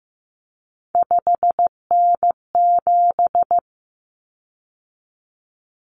Morse code, heard as a keyed tone, about 15 wpm, 700 Hz. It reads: 5N7
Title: Common Cut Numbers - 3-Digit 15wpm